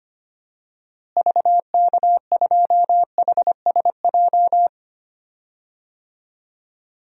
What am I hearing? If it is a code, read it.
Morse code, heard as a keyed tone, about 25 wpm, 700 Hz. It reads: VK2HSJ
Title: DXCC-100 Call Signs - 6 Characters - Repeated 3x - 25wpm